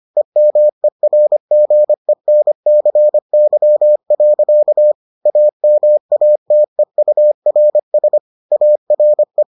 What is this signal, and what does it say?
Morse code, heard as a keyed tone, about 25 wpm, 600 Hz. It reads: EMERGENCY. AMATEURS ARE